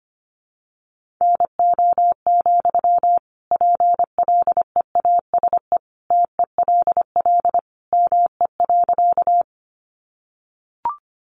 Morse code, heard as a keyed tone, about 25 wpm, 700 Hz. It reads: NO, PLEASE TELL ME.
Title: Sentences from Top 1500 Words 25wpm